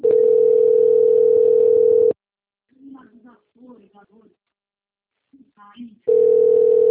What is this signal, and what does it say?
Ringback tone (North America), 440 + 480 Hz, with speech.
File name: out- - - - - .45816